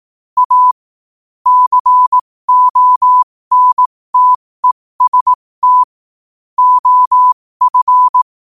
Morse code, heard as a keyed tone, about 18 wpm, spaced as for 15 wpm, 1000 Hz. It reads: A CONTEST OF
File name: qotdfiles / 15wpm